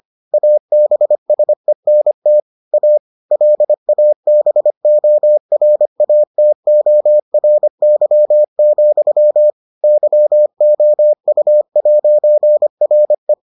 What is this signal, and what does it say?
Morse code, heard as a keyed tone, about 25 wpm, 600 Hz. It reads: ABSENT A LABORATORY, YOU'RE